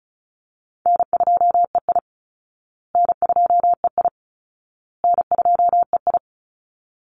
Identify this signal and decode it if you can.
Morse code, heard as a keyed tone, about 34 wpm, 700 Hz. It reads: N2EI N2EI N2EI
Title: Call Signs Repeated 2x - 35wpm